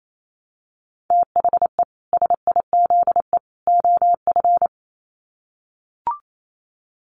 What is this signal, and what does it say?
Morse code, heard as a keyed tone, about 28 wpm, 700 Hz. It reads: THE SIZE OF
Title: Sets of 3 Words 28wpm